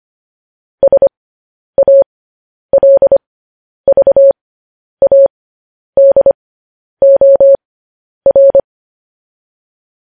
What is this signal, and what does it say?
Morse code, heard as a keyed tone, about 25 wpm, spaced as for 10 wpm, 550 Hz. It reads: SALVADOR
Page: news headlines in morse code at 10 wpm